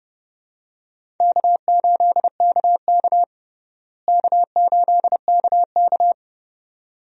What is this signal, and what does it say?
Morse code, heard as a keyed tone, about 30 wpm, 700 Hz. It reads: K8KK K8KK
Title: USA Call Signs - 4 Characters - Repeated 2x - 30wpm